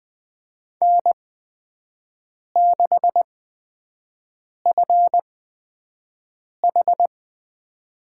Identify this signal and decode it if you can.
Morse code, heard as a keyed tone, about 20 wpm, spaced as for 5 wpm, 700 Hz. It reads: N6FH
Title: USA Call Signs - 4 Characters - 8x Character Spacing - 20wpm